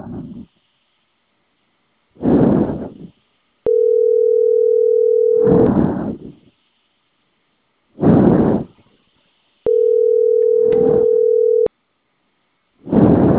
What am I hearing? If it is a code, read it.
Ringback tone (North America), 440 + 480 Hz.
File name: CENTRAL CAEA / out agcaea02 - .31205